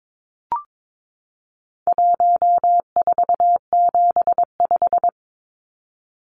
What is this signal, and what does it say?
Morse code, heard as a keyed tone, about 22 wpm, 700 Hz. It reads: 1475
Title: Numbers - 4-Digit 22wpm